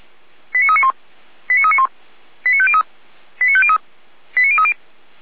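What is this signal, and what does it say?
Five-tone selective call (ZVEI), heard as quick runs of five tones, tones of about 80 ms. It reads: 89391 89391 89503 89503 89030
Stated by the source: Received by SDR.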